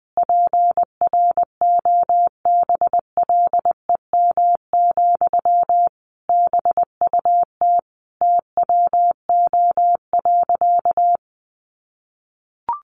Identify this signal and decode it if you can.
Morse code, heard as a keyed tone, about 20 wpm, 700 Hz. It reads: PROBLEM, BUT TWO.
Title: Sentences from Top 1400 Words 20wpm